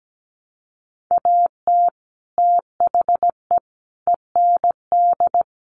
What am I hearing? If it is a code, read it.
Morse code, heard as a keyed tone, about 17 wpm, 700 Hz. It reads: AT THE END
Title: Sets of 5 Words Encore 17wpm